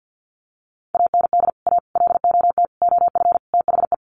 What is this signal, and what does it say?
Morse code, heard as a keyed tone, about 50 wpm, 700 Hz. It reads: AND A LOT OF THE
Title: Sets of 5 Words Encore 50wpm